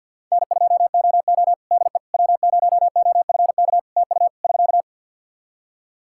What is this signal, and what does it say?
Morse code, heard as a keyed tone, about 50 wpm, 700 Hz. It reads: N2OO DE W0OFK TU <SK>
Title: K1USN SST QSO Practice 50wpm